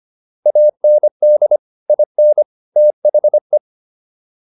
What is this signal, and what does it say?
Morse code, heard as a keyed tone, about 25 wpm, 600 Hz. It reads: AND IN THE